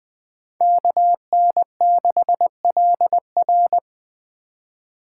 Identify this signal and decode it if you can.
Morse code, heard as a keyed tone, about 21 wpm, 700 Hz. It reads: KN6LR
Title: Speed Racing - USA 2x2 Call Signs 17wpm